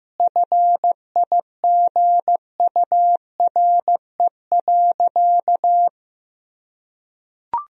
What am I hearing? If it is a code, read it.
Morse code, heard as a keyed tone, about 15 wpm, 700 Hz. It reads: FIGURE.
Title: Sentences from Top 1000 Words - Review Part 4 - 15wpm